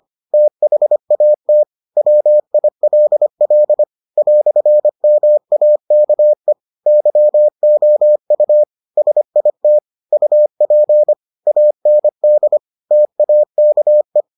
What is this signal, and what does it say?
Morse code, heard as a keyed tone, about 25 wpm, 600 Hz. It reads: THAT WILL "MAKE YOU SIT UP AND TAKE